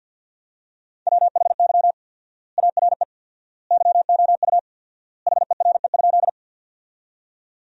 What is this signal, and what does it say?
Morse code, heard as a keyed tone, about 52 wpm, 700 Hz. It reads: WHY ARE YOU HERE?
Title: 4x Word-Spacing - Sentences from Top 200 Words - 50wpm